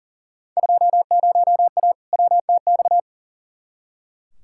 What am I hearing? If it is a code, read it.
Morse code, heard as a keyed tone, about 39 wpm, 700 Hz. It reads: 20A WTX
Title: Field Day Exchanges Encore - 40wpm